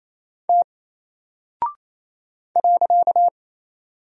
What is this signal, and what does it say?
Morse code, heard as a keyed tone, about 27 wpm, 700 Hz. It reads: T .
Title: Single Character (Letter:Number:Punctuation:Prosign) Alt 28wpm